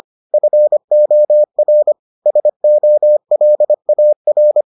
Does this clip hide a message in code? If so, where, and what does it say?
Yes: Morse code from 0.3 to 4.6 s, heard as a keyed tone, about 25 wpm, 600 Hz. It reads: FOR SOLAR